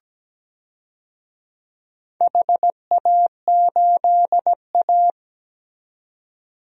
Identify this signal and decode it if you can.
Morse code, heard as a keyed tone, about 17 wpm, 700 Hz. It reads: HA8A
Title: DXCC-100 Call Signs - 4 Characters - 17wpm